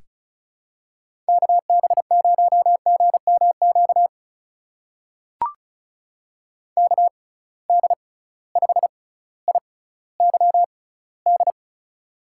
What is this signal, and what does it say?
Morse code, heard as a keyed tone, about 36 wpm, spaced as for 18 wpm, 700 Hz. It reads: KB0GMQ KD5IYD
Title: USA Call Signs - 6 Characters - 6x Character Spacing - 35wpm